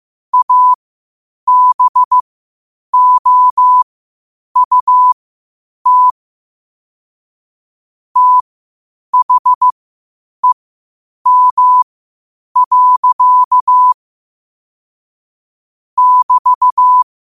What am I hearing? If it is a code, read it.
Morse code, heard as a keyed tone, about 15 wpm, spaced as for 8 wpm, 1000 Hz. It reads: ABOUT THEM. <BT>
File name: qotdfiles / 8wpm